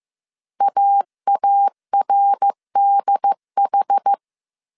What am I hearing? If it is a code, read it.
Morse code, heard as a keyed tone, about 15 wpm, 800 Hz. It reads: AARDH
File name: CW / DET / 1 3 1500C S A 14wpm 2447